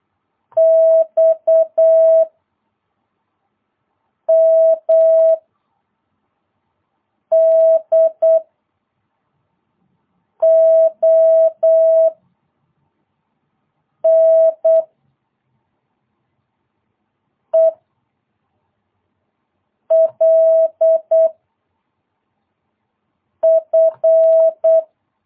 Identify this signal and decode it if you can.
Morse code, heard as a keyed tone, about 8 wpm, 650 Hz. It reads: X M D O N E L F
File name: jh8cbh / 2022cw